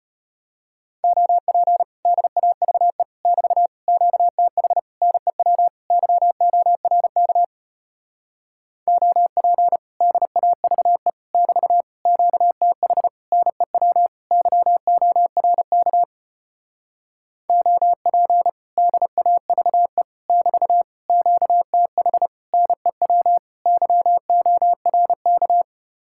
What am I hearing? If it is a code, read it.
Morse code, heard as a keyed tone, about 33 wpm, 700 Hz. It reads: OP DAVE <BT> QTH NEW YORK OP DAVE <BT> QTH NEW YORK OP DAVE <BT> QTH NEW YORK